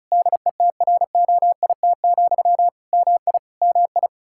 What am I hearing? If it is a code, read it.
Morse code, heard as a keyed tone, about 34 wpm, 700 Hz. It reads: DETROIT, MI MI
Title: Practice QSO - Exchange 1 - Beginning - 35wpm